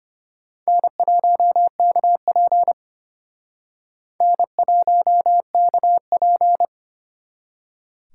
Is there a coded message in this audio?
Yes: Morse code from 0.7 to 6.6 s, heard as a keyed tone, about 32 wpm, 700 Hz. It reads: N1KP N1KP